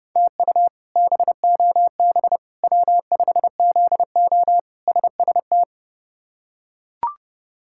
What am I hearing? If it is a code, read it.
Morse code, heard as a keyed tone, about 30 wpm, 700 Hz. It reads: TU BOB W5ZO SST
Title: K1USN SST QSO Practice - 30wpm